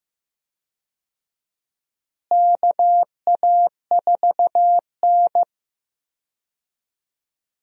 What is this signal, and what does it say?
Morse code, heard as a keyed tone, about 15 wpm, 700 Hz. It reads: KA4N
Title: USA Call Signs - 4 Characters - Repeated 2x - 15wpm